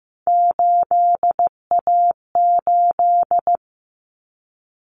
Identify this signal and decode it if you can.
Morse code, heard as a keyed tone, about 15 wpm, 700 Hz. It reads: 8A8
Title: Common Cut Numbers - 3-Digit 15wpm